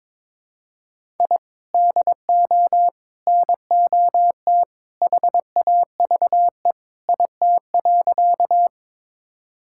Morse code, heard as a keyed tone, about 22 wpm, 700 Hz. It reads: I DO NOT HAVE IT.
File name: practice / 22wpm / Sentences with 3-Word Phrases - 22wpm